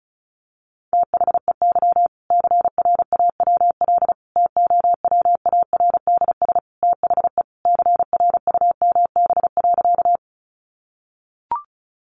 Morse code, heard as a keyed tone, about 34 wpm, 700 Hz. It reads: THEY CRAWL TOWARDS THE CRUMB.
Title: Additional Easy Phrases 35wpm